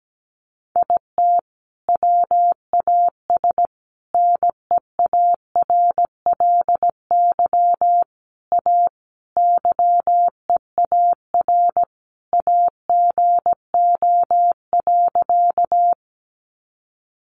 Morse code, heard as a keyed tone, about 17 wpm, 700 Hz. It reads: IT WAS NEARLY A YEAR AGO.
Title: Sentences from Top 1300 Words 17wpm